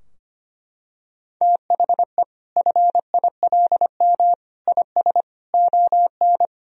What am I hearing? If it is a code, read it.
Morse code, heard as a keyed tone, about 25 wpm, 700 Hz. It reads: THE FILM IS ON
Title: Sentences from Top 1000 Words - Review Part 3 - 25wpm